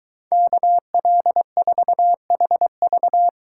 Morse code, heard as a keyed tone, about 22 wpm, 700 Hz. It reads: KL4HV